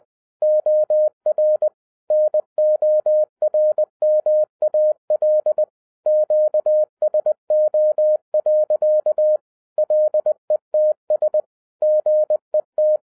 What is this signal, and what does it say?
Morse code, heard as a keyed tone, about 20 wpm, 610 Hz. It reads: OR NORMAL QSO. LETS GET
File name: 2012-01-mp3 / n16-THROUGH EIGHT MONTHS-20wpm